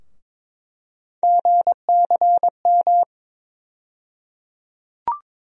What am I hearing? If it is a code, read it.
Morse code, heard as a keyed tone, about 21 wpm, 700 Hz. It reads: GCM
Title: USA Call Sign Postfixes - 3 Letters 22wpm